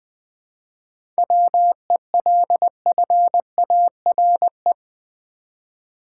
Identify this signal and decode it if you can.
Morse code, heard as a keyed tone, about 20 wpm, 700 Hz. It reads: WELFARE
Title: 7 Letter Words - First 3 Letters Given - 20wpm